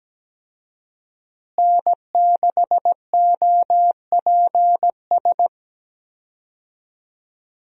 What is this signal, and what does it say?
Morse code, heard as a keyed tone, about 17 wpm, 700 Hz. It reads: N6OPS
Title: USA Call Signs - 5 Characters - Repeated 2x - 17wpm